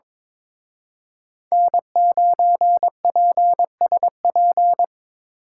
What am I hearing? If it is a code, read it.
Morse code, heard as a keyed tone, about 25 wpm, 700 Hz. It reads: N9PSP
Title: Speed Racing - USA 1x3 Call Signs 22wpm